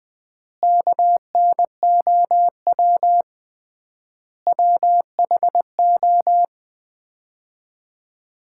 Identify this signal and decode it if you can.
Morse code, heard as a keyed tone, about 20 wpm, 700 Hz. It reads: KNOW WHO
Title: 3x Word-Spacing - Sets of 5 Words - 20wpm